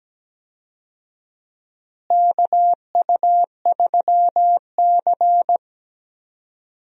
Morse code, heard as a keyed tone, about 17 wpm, 700 Hz. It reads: KU3C
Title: USA Call Signs - 4 Characters - Repeated 3x - 17wpm